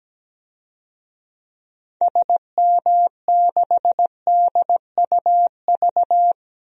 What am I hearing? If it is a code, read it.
Morse code, heard as a keyed tone, about 17 wpm, 700 Hz. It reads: SM6DUV
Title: DXCC-100 Call Signs - 6 Characters - 17wpm